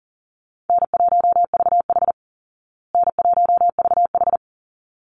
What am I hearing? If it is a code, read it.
Morse code, heard as a keyed tone, about 37 wpm, 700 Hz. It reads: N1VH N1VH
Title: Easy Call Signs Repeated 2x - 40wpm